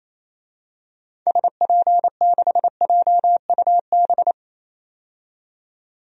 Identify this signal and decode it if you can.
Morse code, heard as a keyed tone, about 28 wpm, 700 Hz. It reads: SP6JUB
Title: DXCC-100 Call Signs - 6 Characters - 28wpm